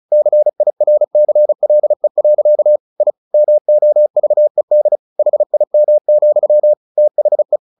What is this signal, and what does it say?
Morse code, heard as a keyed tone, about 34 wpm, 600 Hz. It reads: CIRCLE. I MOVED HIM, THE
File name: the old man and the sea 2 35wpm